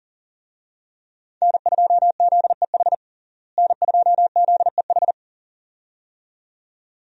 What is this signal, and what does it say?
Morse code, heard as a keyed tone, about 40 wpm, 700 Hz. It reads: N2ZEH N2ZEH
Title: USA Call Signs - 5 Characters - Repeated 2x - 40wpm